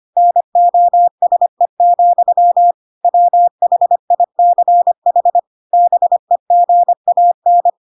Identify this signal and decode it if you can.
Morse code, heard as a keyed tone, about 25 wpm, 700 Hz. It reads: NOSE, WHICH BEGAN
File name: Pinocchio / Pinocchio ch3 25 25 ik1hge